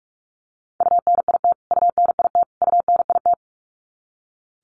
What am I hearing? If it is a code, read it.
Morse code, heard as a keyed tone, about 44 wpm, 700 Hz. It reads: UNIT UNIT UNIT
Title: Top 500 Words Repeated 45wpm